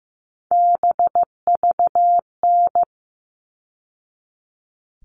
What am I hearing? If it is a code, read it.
Morse code, heard as a keyed tone, about 15 wpm, 700 Hz. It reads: BVN
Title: All Cut Numbers - 3-Digit 15wpm